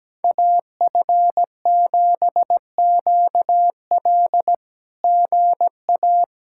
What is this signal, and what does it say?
Morse code, heard as a keyed tone, about 17 wpm, 700 Hz. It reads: AF7QL GA
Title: SOTA QSO Practice 17wpm